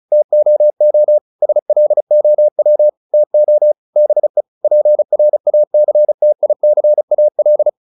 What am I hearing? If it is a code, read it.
Morse code, heard as a keyed tone, about 34 wpm, 600 Hz. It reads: TOO SLOW TO BE PRACTICAL